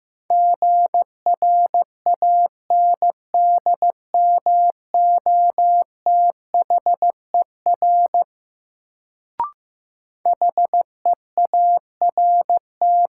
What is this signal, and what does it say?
Morse code, heard as a keyed tone, about 15 wpm, 700 Hz. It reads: GRANDMOTHER HEART
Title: Sentences from Top 1200 Words - New Word Drill - 15wpm